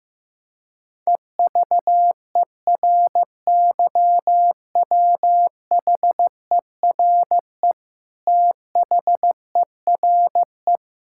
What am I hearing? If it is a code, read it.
Morse code, heard as a keyed tone, about 15 wpm, 700 Hz. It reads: EVERYWHERE THERE
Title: Sentences from Top 1600 Words 15wpm